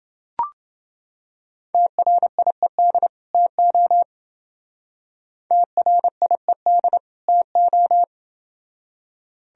Morse code, heard as a keyed tone, about 29 wpm, 700 Hz. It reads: TRIED TO TRIED TO